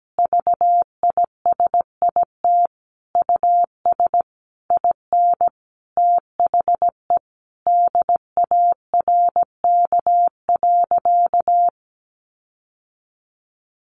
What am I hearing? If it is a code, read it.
Morse code, heard as a keyed tone, about 17 wpm, 700 Hz. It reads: VISIT US IN THE DARK.
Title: Sentences from Top 1400 Words 17wpm